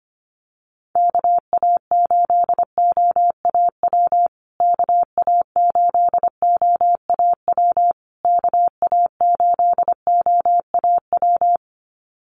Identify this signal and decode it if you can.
Morse code, heard as a keyed tone, about 25 wpm, 700 Hz. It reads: KA8OAW KA8OAW KA8OAW
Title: USA Call Signs Repeated 3x 25wpm